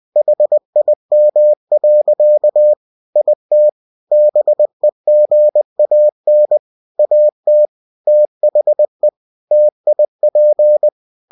Morse code, heard as a keyed tone, about 20 wpm, 600 Hz. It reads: HIM. IT BEGAN AT THE TIP
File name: cw-books / winnie-the-pooh / winnie-20wpm-page-0049